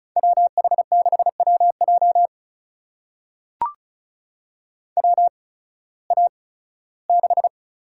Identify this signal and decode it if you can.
Morse code, heard as a keyed tone, about 36 wpm, spaced as for 19 wpm, 700 Hz. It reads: WH6WJ WA6
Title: USA Call Signs - 5 Characters - 8x Character Spacing - 35wpm